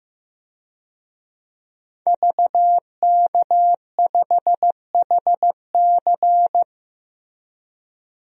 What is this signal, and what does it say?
Morse code, heard as a keyed tone, about 15 wpm, 700 Hz. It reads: VK5HC
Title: DXCC-100 Call Signs - 5 Characters - Repeated 3x - 15wpm